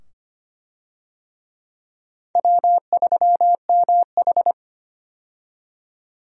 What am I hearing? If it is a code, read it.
Morse code, heard as a keyed tone, about 25 wpm, 700 Hz. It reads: W3MH
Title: USA Call Signs - 4 Characters - 25wpm